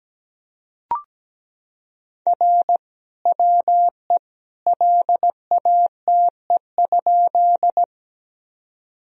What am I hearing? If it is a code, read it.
Morse code, heard as a keyed tone, about 17 wpm, 700 Hz. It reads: R WE LATE?
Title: Sentences from Top 300 Words and Abbreviations 17wpm